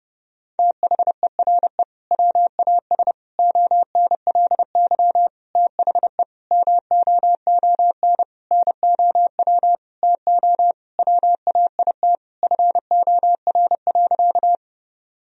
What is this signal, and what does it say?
Morse code, heard as a keyed tone, about 30 wpm, 700 Hz. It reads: THERE WAS ONLY THE MOON NOW TO WAIT FOR.
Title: Sentences from Top 1000 Words - Review Part 3 - 30wpm